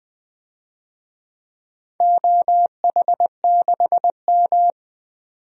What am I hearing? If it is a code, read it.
Morse code, heard as a keyed tone, about 20 wpm, 700 Hz. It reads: OH6M
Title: DXCC-100 Call Signs - 4 Characters - 20wpm